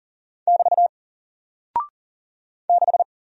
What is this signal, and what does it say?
Morse code, heard as a keyed tone, about 40 wpm, 700 Hz. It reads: <BT> 6
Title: Single Character (Letter:Number:Punctuation:Prosign) Alt 40wpm